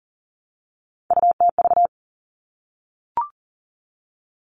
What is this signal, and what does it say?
Morse code, heard as a keyed tone, about 40 wpm, 700 Hz. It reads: UTV
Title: All Cut Numbers - 3-Digit 40wpm